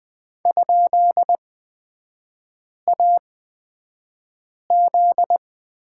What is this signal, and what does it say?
Morse code, heard as a keyed tone, about 20 wpm, spaced as for 5 wpm, 700 Hz. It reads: ?AZ
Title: 243 Lesson 39e - Letter z - Three Characters 20wpm